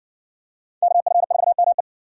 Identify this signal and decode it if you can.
Morse code, heard as a keyed tone, about 53 wpm, 700 Hz. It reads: KF4ME